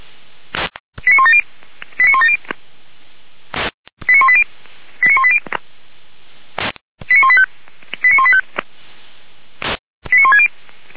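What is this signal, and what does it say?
Five-tone selective call (ZVEI), heard as quick runs of five tones, tones of about 70 ms. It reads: 89170 89170 89180 89180 89185 89185 89160
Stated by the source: Received on an SDR, ZVEI five-tone selective call, code 89133, from the Feuerwehr Orlamünde fire brigade audio server.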